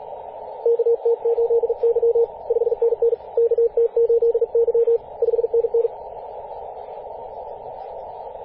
Morse code, heard as a keyed tone, about 44 wpm, 470 Hz, with radio noise. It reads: KT8Y 5NN KT8Y 5NN
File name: posts / 2016-apr-22-2153z-kh8-kc0w-18-080mhz-cw